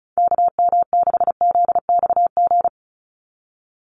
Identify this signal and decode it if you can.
Morse code, heard as a keyed tone, about 34 wpm, 700 Hz. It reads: KM6ZXG
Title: USA Call Signs 35wpm